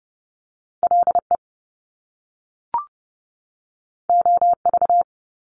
Morse code, heard as a keyed tone, about 30 wpm, 700 Hz. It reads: LE OV
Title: Top Two Letter Combinations in Words 30wpm